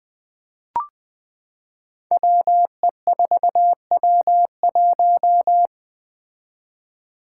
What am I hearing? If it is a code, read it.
Morse code, heard as a keyed tone, about 20 wpm, 700 Hz. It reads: WE4W1
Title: Five Challenging Letter-Number Combinations 20wpm